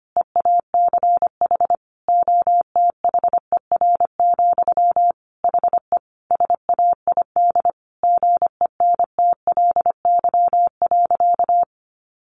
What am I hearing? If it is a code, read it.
Morse code, heard as a keyed tone, about 25 wpm, 700 Hz. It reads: EACH OTHER, HE SAID GENTLY.
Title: Sentences from Top 1500 Words 25wpm